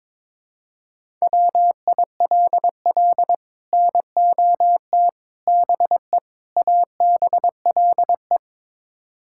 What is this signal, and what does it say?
Morse code, heard as a keyed tone, about 22 wpm, 700 Hz. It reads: WILL NOT BE ABLE
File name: practice / 22wpm / Build A Phrase - 5-Word Phrases - 22wpm